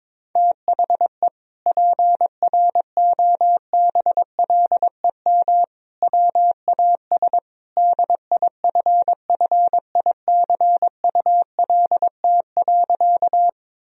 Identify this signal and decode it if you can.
Morse code, heard as a keyed tone, about 22 wpm, 700 Hz. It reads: THE PROBLEM WAS DIFFICULT.
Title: Sentences from Top 1000 Words - Review Part 2 - 22wpm